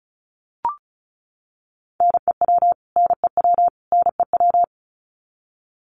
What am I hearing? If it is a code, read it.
Morse code, heard as a keyed tone, about 34 wpm, 700 Hz. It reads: NEW NEW NEW
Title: Top 500 Words Repeated 35wpm